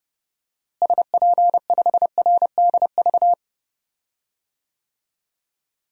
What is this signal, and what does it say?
Morse code, heard as a keyed tone, about 30 wpm, 700 Hz. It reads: SP5RDV
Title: DXCC-100 Call Signs - 6 Characters - Repeated 3x - 30wpm